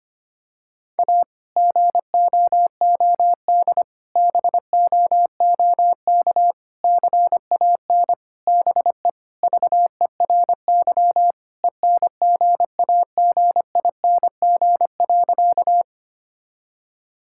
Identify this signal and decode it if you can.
Morse code, heard as a keyed tone, about 25 wpm, 700 Hz. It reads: A GOOD BOOK CAN BE VERY ENGAGING.
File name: practice / 25wpm / Sentences with 2-Word Phrases - 25wpm